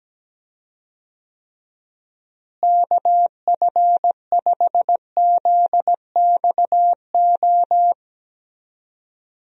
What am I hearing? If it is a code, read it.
Morse code, heard as a keyed tone, about 17 wpm, 700 Hz. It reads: KF5ZXO
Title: USA Call Signs - 6 Characters - Repeated 3x - 17wpm